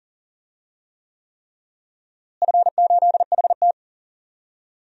Morse code, heard as a keyed tone, about 40 wpm, 700 Hz. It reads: F8HT
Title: DXCC-100 Call Signs - 4 Characters - Repeated 3x - 40wpm